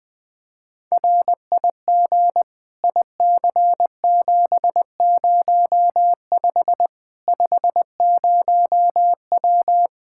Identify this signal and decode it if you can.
Morse code, heard as a keyed tone, about 20 wpm, 700 Hz. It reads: RIG IC705 50W
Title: Practice QSO - Exchange 2 - Info - 20wpm